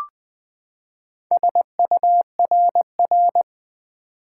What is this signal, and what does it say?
Morse code, heard as a keyed tone, about 20 wpm, 700 Hz. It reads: SURR
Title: Build A Word - 4 Letters At A Time - 8-Letter Words - 20wpm